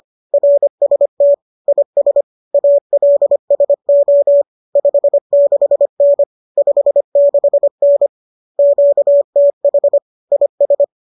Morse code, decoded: RST IS ALSO 56N 56N QTH IS

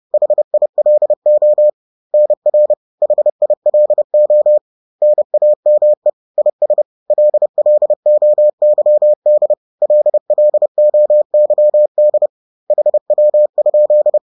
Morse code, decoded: HILO NR HILO NAME IS LLOYD LLOYD HW?

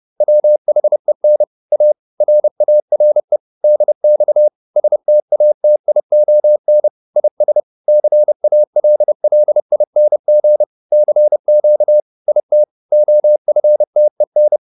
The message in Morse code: WHEN A RARE DX STATION IS CALLING CQ IT OFTEN

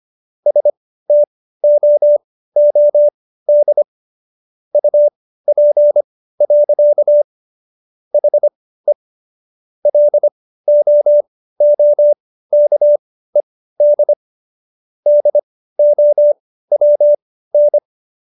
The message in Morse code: STOOD UP. HE LOOKED DOWN

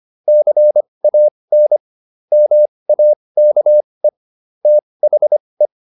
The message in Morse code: CAN MAKE THE

